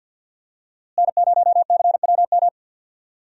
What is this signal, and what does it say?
Morse code, heard as a keyed tone, about 49 wpm, 700 Hz. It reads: N0CWM
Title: USA Call Signs - 5 Characters - Repeated 3x - 50wpm